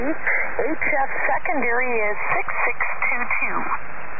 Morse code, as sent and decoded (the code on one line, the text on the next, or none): none